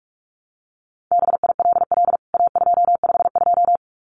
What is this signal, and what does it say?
Morse code, heard as a keyed tone, about 44 wpm, 700 Hz. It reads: BILL A252
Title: CWT Exchange 45wpm